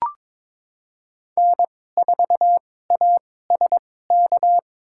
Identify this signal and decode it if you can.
Morse code, heard as a keyed tone, about 23 wpm, spaced as for 16 wpm, 700 Hz. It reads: N4ASK
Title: USA Call Signs - 5 Characters - 2x Character Spacing - 22wpm